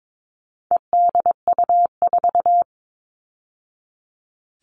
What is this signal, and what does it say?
Morse code, heard as a keyed tone, about 22 wpm, 700 Hz. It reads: EDU4
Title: All Cut Numbers - 4-Digit 22wpm